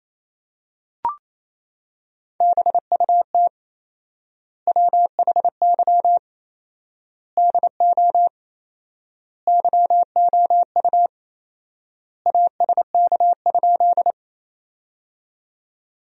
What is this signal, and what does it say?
Morse code, heard as a keyed tone, about 28 wpm, 700 Hz. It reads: BUT WHY DO YOU ASK?
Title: 4x Word-Spacing - Sentences from Top 200 Words - 28wpm